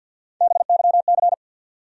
Morse code, heard as a keyed tone, about 48 wpm, 700 Hz. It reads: BYC
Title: USA Call Sign Postfixes - 3 Letters 50wpm